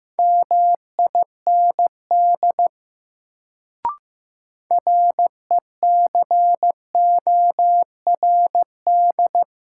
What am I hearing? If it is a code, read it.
Morse code, heard as a keyed tone, about 15 wpm, 700 Hz. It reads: MIND RECORD
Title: Sentences from Top 500 Words - New Word Drill - 15wpm